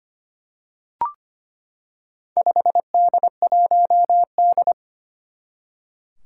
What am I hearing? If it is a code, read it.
Morse code, heard as a keyed tone, about 25 wpm, 700 Hz. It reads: 5D1D
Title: Four Challenging Letter-Number Combinations 25wpm